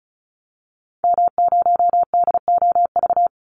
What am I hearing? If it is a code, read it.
Morse code, heard as a keyed tone, about 34 wpm, 700 Hz. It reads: M0DOV
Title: Call Signs Repeated 2x - 35wpm